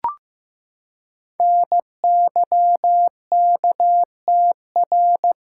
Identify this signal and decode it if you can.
Morse code, heard as a keyed tone, about 15 wpm, 700 Hz. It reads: NYKTR
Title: Code Groups - 5 Letters - 15wpm